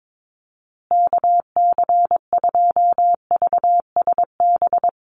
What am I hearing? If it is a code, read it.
Morse code, heard as a keyed tone, about 22 wpm, 700 Hz. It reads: KC2VSB
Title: USA Call Signs Repeated 3x 22wpm